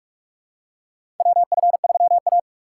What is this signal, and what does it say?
Morse code, heard as a keyed tone, about 46 wpm, 690 Hz. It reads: WF3A
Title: USA Call Signs - 4 Characters - 3x Character Spacing - 45wpm